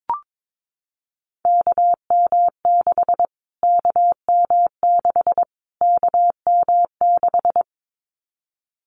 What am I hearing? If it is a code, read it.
Morse code, heard as a keyed tone, about 22 wpm, 700 Hz. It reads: KM6 KM6 KM6
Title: Callsign Prefixes for Top 25 DXCC - Repeated 3x - 22wpm